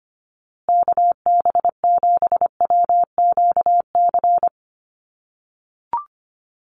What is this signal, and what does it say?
Morse code, heard as a keyed tone, about 25 wpm, 700 Hz. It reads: KB7WQC
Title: USA Call Signs Repeated 2x 25wpm